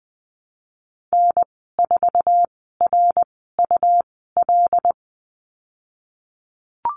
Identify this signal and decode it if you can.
Morse code, heard as a keyed tone, about 20 wpm, spaced as for 15 wpm, 700 Hz. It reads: N4RUL
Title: 186 Lesson 30g - Letter b - Call Signs 20wpm